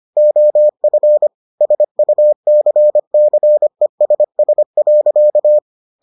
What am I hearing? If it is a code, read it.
Morse code, heard as a keyed tone, about 25 wpm, 600 Hz. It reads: OF SUCCESS.